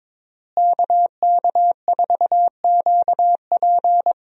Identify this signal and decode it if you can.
Morse code, heard as a keyed tone, about 22 wpm, 700 Hz. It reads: KK4QP
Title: USA Call Signs - 5 Characters - Repeated 3x - 22wpm